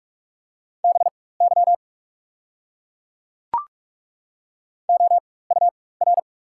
Morse code, heard as a keyed tone, about 43 wpm, spaced as for 21 wpm, 690 Hz. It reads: BY OUR
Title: 4x Character Spacing - Top 200 Words - 45wpm